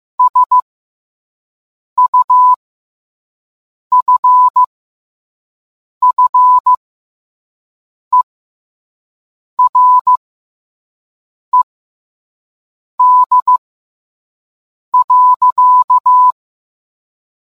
Morse code, heard as a keyed tone, about 15 wpm, spaced as for 5 wpm, 1000 Hz. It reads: SUFFERED.